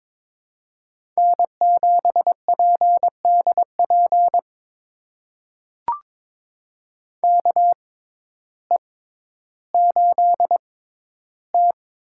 Morse code, heard as a keyed tone, about 21 wpm, spaced as for 12 wpm, 700 Hz. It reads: N7PDP KE8T
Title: USA Call Signs - 5 Characters - 6x Character Spacing - 22wpm